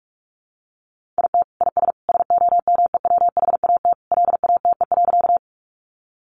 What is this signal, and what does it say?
Morse code, heard as a keyed tone, about 44 wpm, 700 Hz. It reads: IT IS SOMEWHAT LATE.
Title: Sentences from Top 1300 Words 45wpm